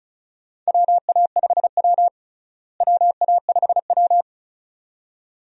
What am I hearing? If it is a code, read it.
Morse code, heard as a keyed tone, about 34 wpm, 700 Hz. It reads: WA5W WA5W
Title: USA Call Signs - 4 Characters - Repeated 2x - 35wpm